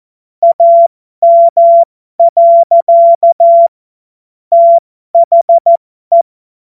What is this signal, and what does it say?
Morse code, heard as a keyed tone, about 14 wpm, spaced as for 12 wpm, 680 Hz. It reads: AM. THE